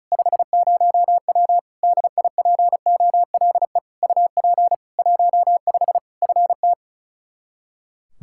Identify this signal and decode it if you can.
Morse code, heard as a keyed tone, about 35 wpm, 700 Hz. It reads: 50W DIPOLE UP 15 FT
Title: K1USN SST QSO Practice 35wpm